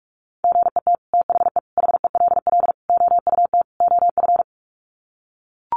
Morse code, heard as a keyed tone, about 44 wpm, 700 Hz. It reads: GET THE HELL OUT OF
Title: Sets of 5 Words Encore2 45wpm